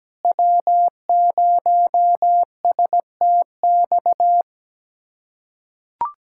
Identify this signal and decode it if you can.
Morse code, heard as a keyed tone, about 17 wpm, 700 Hz. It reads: W0STX
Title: USA Callsign DE Callsign - 5 Characters - 17wpm